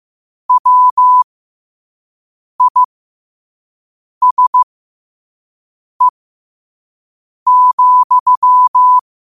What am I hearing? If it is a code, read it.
Morse code, heard as a keyed tone, about 15 wpm, spaced as for 5 wpm, 1000 Hz. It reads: WISE,
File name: qotdfiles / 5wpm